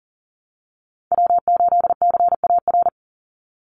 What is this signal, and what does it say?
Morse code, heard as a keyed tone, about 40 wpm, 700 Hz. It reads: W8CAR
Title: Call Signs - 40wpm